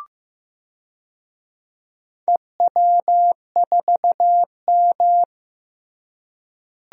Morse code, heard as a keyed tone, about 15 wpm, 700 Hz. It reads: EW4M
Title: DXCC-100 Call Signs - 4 Characters - Repeated 3x - 15wpm